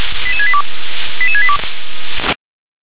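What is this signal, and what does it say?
Five-tone selective call (ZVEI), heard as quick runs of five tones, tones of about 80 ms. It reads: 88682 88682
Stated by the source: ZVEI five-tone call 88148 received on an SDR.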